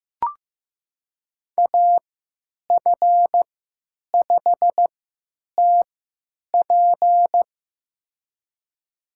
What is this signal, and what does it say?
Morse code, heard as a keyed tone, about 16 wpm, spaced as for 9 wpm, 700 Hz. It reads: AF5TP
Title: USA Call Signs - 3x Character Spacing - 15wpm